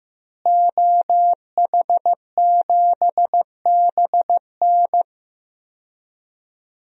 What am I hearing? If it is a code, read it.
Morse code, heard as a keyed tone, about 15 wpm, 700 Hz. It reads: OH7BN
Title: DXCC-100 Call Signs - 5 Characters - Repeated 2x - 15wpm